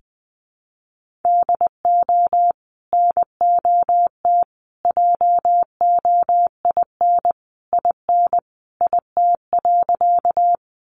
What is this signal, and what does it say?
Morse code, heard as a keyed tone, about 20 wpm, 700 Hz. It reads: DO NOT JOIN IN IT.
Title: Sentences from Top 1400 Words 20wpm